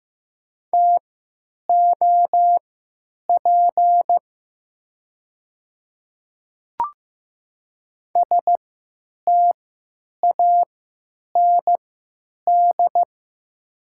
Morse code, heard as a keyed tone, about 15 wpm, spaced as for 9 wpm, 700 Hz. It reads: TOP STAND
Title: 3x Character Spacing - Top 500 Words - 15wpm